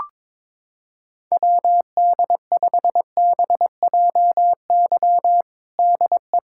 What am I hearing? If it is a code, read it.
Morse code, heard as a keyed tone, about 22 wpm, 700 Hz. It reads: WD5BJY DE
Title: USA Callsign DE Callsign - 6 Characters - 22wpm